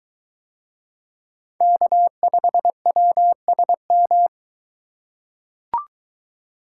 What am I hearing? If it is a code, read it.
Morse code, heard as a keyed tone, about 22 wpm, 700 Hz. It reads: K5WSM